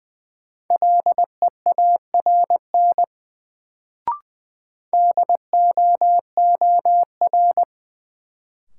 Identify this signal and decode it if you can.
Morse code, heard as a keyed tone, about 20 wpm, 700 Hz. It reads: LEARN DOOR
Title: Sentences from Top 300 Words - New Word Drill - 20wpm